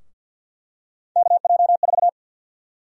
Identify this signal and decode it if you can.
Morse code, heard as a keyed tone, about 49 wpm, 700 Hz. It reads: KJ4
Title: USA Call Sign Prefixes - 50wpm